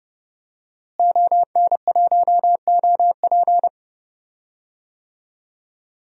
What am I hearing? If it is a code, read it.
Morse code, heard as a keyed tone, about 31 wpm, 700 Hz. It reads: ON1OP
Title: DXCC-100 Call Signs - 5 Characters - Repeated 2x - 30wpm